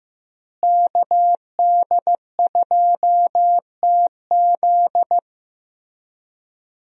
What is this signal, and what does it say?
Morse code, heard as a keyed tone, about 15 wpm, 700 Hz. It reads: KD2TZ
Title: USA Call Signs - 5 Characters - Repeated 3x - 15wpm